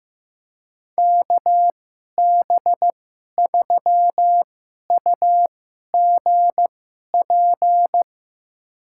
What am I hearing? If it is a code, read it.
Morse code, heard as a keyed tone, about 15 wpm, spaced as for 11 wpm, 700 Hz. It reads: KB3UGP
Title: USA Call Signs - 2x Character Spacing - 15wpm